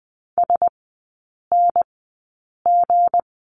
Morse code, heard as a keyed tone, about 20 wpm, spaced as for 8 wpm, 690 Hz. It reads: SNG
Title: 159 Lesson 26e - Letter g - Three Characters 20wpm